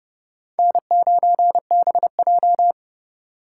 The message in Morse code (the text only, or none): N9BJ